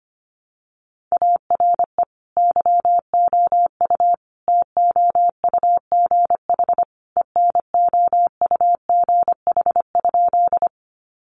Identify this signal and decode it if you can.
Morse code, heard as a keyed tone, about 25 wpm, 700 Hz. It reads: ARE YOU TOUGH ENOUGH?
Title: Sentences from Top 1200 Words 25wpm